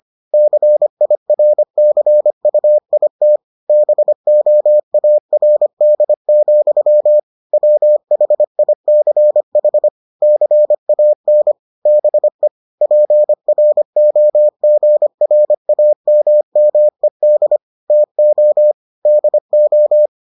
Morse code, heard as a keyed tone, about 25 wpm, 600 Hz. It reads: CIRCUIT BOARD, WHICH CAN BE PROGRAMMED TO DO